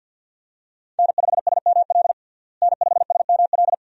Morse code, heard as a keyed tone, about 53 wpm, 700 Hz. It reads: N5SML N5SML
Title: USA Call Signs - 5 Characters - Repeated 2x - 50wpm